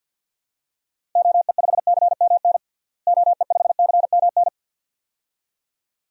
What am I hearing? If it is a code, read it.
Morse code, heard as a keyed tone, about 52 wpm, 700 Hz. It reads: OE5CMN OE5CMN
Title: DXCC-100 Call Signs - 6 Characters - Repeated 2x - 50wpm